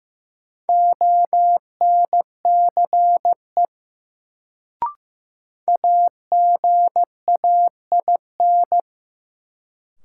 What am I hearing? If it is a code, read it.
Morse code, heard as a keyed tone, about 15 wpm, 700 Hz. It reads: ONCE AGAIN
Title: Sets of 4 Words - New Word Drill - 15wpm